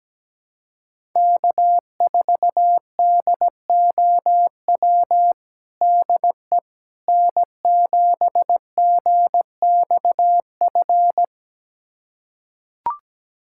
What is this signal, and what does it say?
Morse code, heard as a keyed tone, about 17 wpm, 700 Hz. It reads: K4DOW DE N7GXF
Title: USA Callsign DE Callsign - 5 Characters - 17wpm